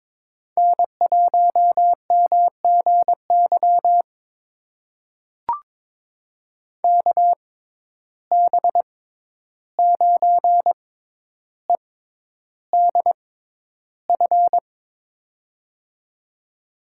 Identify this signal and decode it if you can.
Morse code, heard as a keyed tone, about 22 wpm, spaced as for 11 wpm, 700 Hz. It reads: N1MGY KB9EDF
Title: USA Call Signs - 6x Character Spacing - 22wpm